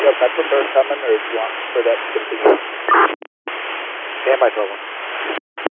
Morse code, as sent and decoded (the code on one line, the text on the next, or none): none